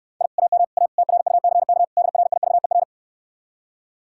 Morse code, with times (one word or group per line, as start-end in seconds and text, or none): 0.2-0.3 I
0.4-0.6 AM
0.8-0.9 A
1.0-1.8 TRUCK
2.0-2.8 DRIVER